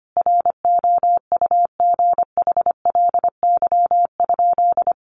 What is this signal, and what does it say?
Morse code, heard as a keyed tone, about 25 wpm, 700 Hz. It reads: ROUGHLY?
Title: Sentences from Top 1500 Words 25wpm